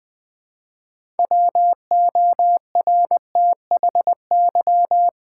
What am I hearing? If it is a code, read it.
Morse code, heard as a keyed tone, about 20 wpm, 700 Hz. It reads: WORTHY
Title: 6 Letter Words - First 3 Letters Given - 20wpm